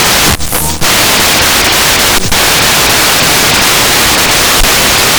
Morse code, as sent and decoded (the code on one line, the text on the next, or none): none